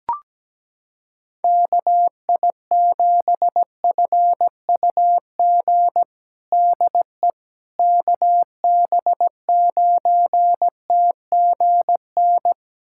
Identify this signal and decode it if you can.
Morse code, heard as a keyed tone, about 17 wpm, 700 Hz. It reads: KI7FUG DE KB9TGN
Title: USA Callsign DE Callsign - 6 Characters - 17wpm